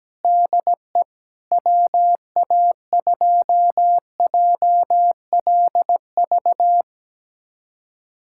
Morse code, heard as a keyed tone, about 17 wpm, 700 Hz. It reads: DE WA2JLV
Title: USA Callsign DE Callsign - 6 Characters - 17wpm